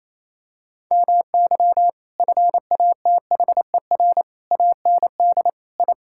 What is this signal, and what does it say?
Morse code, heard as a keyed tone, about 28 wpm, 700 Hz. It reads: MY FATHER AND I
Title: Sentences from Top 1000 Words - Review Part 6 - 28wpm